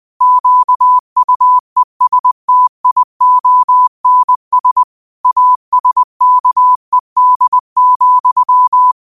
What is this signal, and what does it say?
Morse code, heard as a keyed tone, about 20 wpm, 1000 Hz. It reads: QUESTIONS ASKED,
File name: qotdfiles / 20wpm